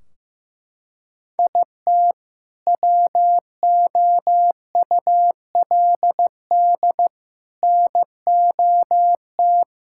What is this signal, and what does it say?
Morse code, heard as a keyed tone, about 16 wpm, 700 Hz. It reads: IT WOULD NOT
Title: Sentences from Top 1600 Words 15wpm